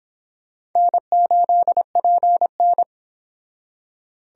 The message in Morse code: N8PN

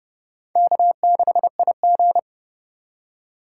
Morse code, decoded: K6IG

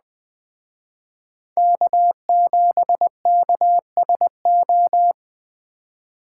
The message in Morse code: K7KSO

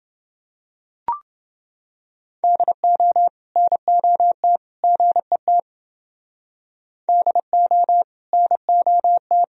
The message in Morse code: DO NOT GET DO NOT

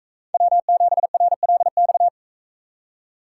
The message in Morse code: W7RLX

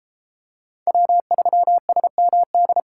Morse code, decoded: W3SMD

